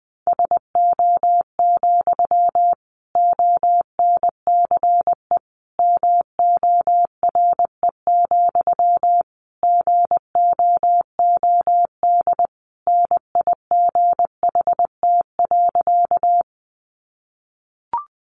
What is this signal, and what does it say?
Morse code, heard as a keyed tone, about 20 wpm, 700 Hz. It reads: SO, ONCE MORE, GOOD NIGHT.
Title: Sentences from Top 300 Words 20wpm